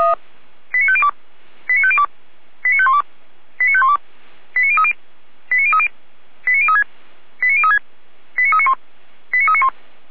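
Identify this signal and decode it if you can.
Five-tone selective call (ZVEI), heard as quick runs of five tones, tones of about 70 ms. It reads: 89502 89502 89512 89512 89030 89030 89036 89036 89391 89391
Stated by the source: ZVEI five-tone call 89125 received on an SDR.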